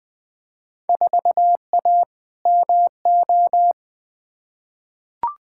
Morse code, heard as a keyed tone, about 21 wpm, 700 Hz. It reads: 4A MO
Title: Field Day Exchanges - 20wpm